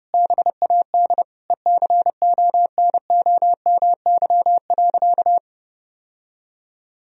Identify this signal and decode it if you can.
Morse code, heard as a keyed tone, about 30 wpm, 700 Hz. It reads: BAD ECONOMY.
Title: Sentences from Top 1000 Words - Review Part 3 - 30wpm